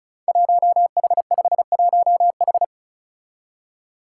Morse code, heard as a keyed tone, about 36 wpm, 700 Hz. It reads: 1H51H
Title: Five Challenging Letter-Number Combinations 35wpm